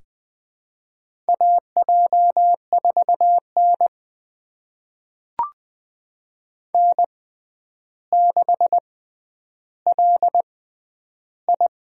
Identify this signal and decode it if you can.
Morse code, heard as a keyed tone, about 20 wpm, spaced as for 10 wpm, 700 Hz. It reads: AJ4N N6LI
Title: USA Call Signs - 4 Characters - 6x Character Spacing - 20wpm